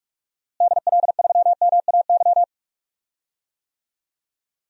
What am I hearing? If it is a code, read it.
Morse code, heard as a keyed tone, about 44 wpm, 700 Hz. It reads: DL3MAY
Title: DXCC-100 Call Signs - 6 Characters - Repeated 3x - 45wpm